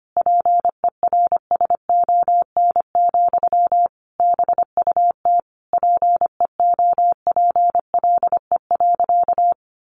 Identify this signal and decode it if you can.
Morse code, heard as a keyed tone, about 25 wpm, 700 Hz. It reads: PERSON, BUT PEOPLE.
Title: Sentences from Top 1400 Words 25wpm